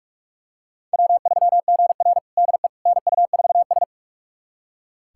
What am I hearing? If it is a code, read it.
Morse code, heard as a keyed tone, about 46 wpm, 700 Hz. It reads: W3GR DE NU4S
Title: USA Callsign DE Callsign - 4 Characters - 45wpm